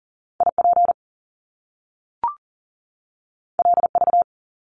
Morse code, heard as a keyed tone, about 38 wpm, 700 Hz. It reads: IP LV
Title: Top Two Letter Combinations in Words 40wpm